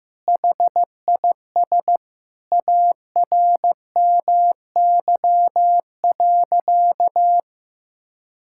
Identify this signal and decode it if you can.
Morse code, heard as a keyed tone, about 15 wpm, 700 Hz. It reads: HIS ARMY.
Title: Sentences from Top 1000 Words - Review Part 6 - 15wpm